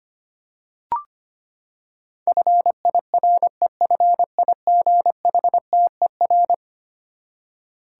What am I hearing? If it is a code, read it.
Morse code, heard as a keyed tone, about 25 wpm, 700 Hz. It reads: FIREFIGHTER
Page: morse-code-ninja-practice-25wpm